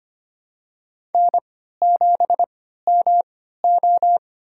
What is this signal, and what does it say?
Morse code, heard as a keyed tone, about 25 wpm, spaced as for 14 wpm, 700 Hz. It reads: N7MO
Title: USA Call Signs - 3x Character Spacing - 25wpm